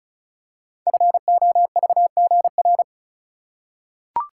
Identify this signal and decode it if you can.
Morse code, heard as a keyed tone, about 36 wpm, 700 Hz. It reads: FOVGR
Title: Code Groups - 5 Letters - 35wpm